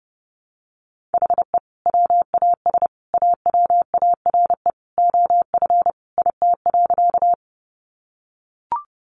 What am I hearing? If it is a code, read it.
Morse code, heard as a keyed tone, about 30 wpm, 700 Hz. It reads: HE WAS AWARE OF IT.
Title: Sentences from Top 1200 Words 30wpm